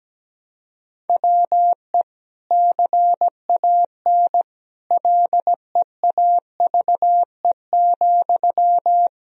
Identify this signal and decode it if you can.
Morse code, heard as a keyed tone, about 17 wpm, 700 Hz. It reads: WE CAN LEAVE,
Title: Sentences from Top 1000 Words - Review Part 4 - 17wpm